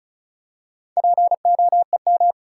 Morse code, heard as a keyed tone, about 36 wpm, 700 Hz. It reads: POEM
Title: Four Letter Words - 35wpm